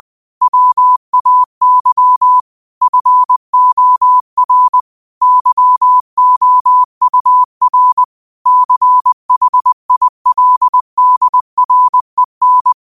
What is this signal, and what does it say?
Morse code, heard as a keyed tone, about 20 wpm, 1000 Hz. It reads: WAY FOR YOUR CHILDREN